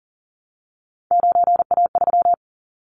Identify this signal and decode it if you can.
Morse code, heard as a keyed tone, about 40 wpm, 700 Hz. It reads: 9A3
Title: Callsign Prefixes for Top 25 DXCC - Repeated 3x - 40wpm